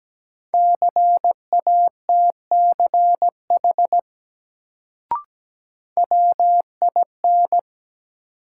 Morse code, heard as a keyed tone, about 17 wpm, 700 Hz. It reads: CATCH WIN
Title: Sentences from Top 1500 Words - New Word Drill Alt - 17wpm